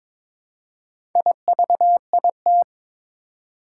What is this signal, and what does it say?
Morse code, heard as a keyed tone, about 22 wpm, 700 Hz. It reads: IVIT
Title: Four Challenging Letter-Number Combinations 22wpm